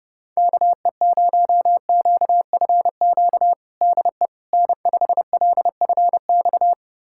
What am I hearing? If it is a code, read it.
Morse code, heard as a keyed tone, about 30 wpm, 700 Hz. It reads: KE0QFQ DE N5LFX
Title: USA Callsign DE Callsign 30wpm